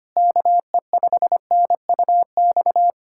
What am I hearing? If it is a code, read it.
Morse code, heard as a keyed tone, about 25 wpm, 700 Hz. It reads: KE5NUX
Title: USA Callsign DE Callsign 25wpm